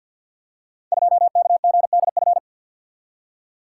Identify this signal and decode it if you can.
Morse code, heard as a keyed tone, about 50 wpm, 700 Hz. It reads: 2KGDF
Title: Code Groups - 5 Letter:Numbers - 50wpm